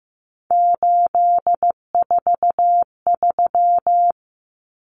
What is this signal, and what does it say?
Morse code, heard as a keyed tone, about 15 wpm, 700 Hz. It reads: 843
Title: Common Cut Numbers - 3-Digit 15wpm